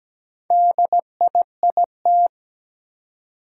Morse code, heard as a keyed tone, about 17 wpm, 700 Hz. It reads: DIIT
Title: Four Challenging Letter-Number Combinations 17wpm